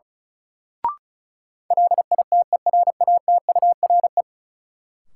Morse code, heard as a keyed tone, about 36 wpm, 700 Hz. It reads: LITERATURE
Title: Sentences from Top 600 Words - New Word Drill - 35wpm